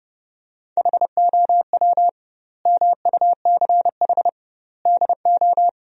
Morse code, heard as a keyed tone, about 30 wpm, 700 Hz. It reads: HOW MUCH DO
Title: 2x Word-Spacing - Sentences from Top 200 Words - 30wpm